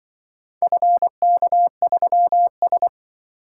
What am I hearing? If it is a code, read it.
Morse code, heard as a keyed tone, about 23 wpm, 700 Hz. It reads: FK3S
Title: Speed-Racing - Four Letter-Number Combinations - 20wpm